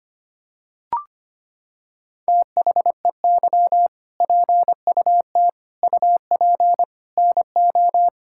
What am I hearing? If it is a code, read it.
Morse code, heard as a keyed tone, about 25 wpm, 700 Hz. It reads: THEY PUT UP NO
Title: Sentences from Top 1600 Words 25wpm